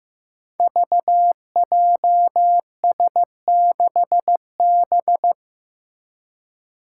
Morse code, heard as a keyed tone, about 15 wpm, 700 Hz. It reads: VJS6B
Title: Five Challenging Letter-Number Combinations 15wpm